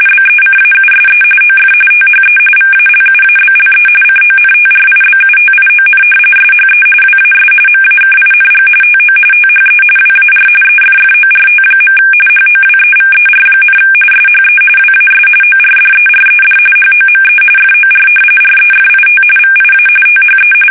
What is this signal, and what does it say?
RTTY, about 45.45 baud. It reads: BECAUSE OF THE VALUABLE CONTRIBUTION SHE WAS MAKING TO THE WAR EFFORT. HOWEVER, SHE WAS ORDERED HOME FOR REFIT VIA THE
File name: fsk850 / battleshipnj